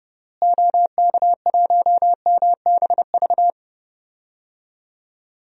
Morse code, heard as a keyed tone, about 30 wpm, 700 Hz. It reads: OK1MBV
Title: DXCC-100 Call Signs - 6 Characters - Repeated 3x - 30wpm